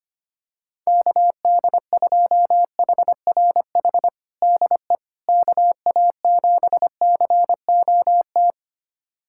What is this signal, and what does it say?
Morse code, heard as a keyed tone, about 25 wpm, 700 Hz. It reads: KD2HRH DE KA7COT
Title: USA Callsign DE Callsign 25wpm